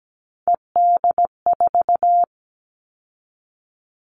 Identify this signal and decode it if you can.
Morse code, heard as a keyed tone, about 17 wpm, 700 Hz. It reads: ED4
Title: All Cut Numbers - 3-Digit 17wpm